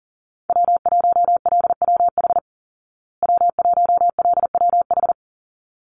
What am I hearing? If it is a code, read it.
Morse code, heard as a keyed tone, about 40 wpm, 700 Hz. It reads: W1LWH W1LWH
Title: Call Signs Repeated 3x - 40wpm